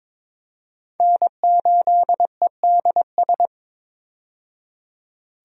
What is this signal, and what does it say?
Morse code, heard as a keyed tone, about 22 wpm, 700 Hz. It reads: N8EDS
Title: USA Call Signs - 5 Characters - 22wpm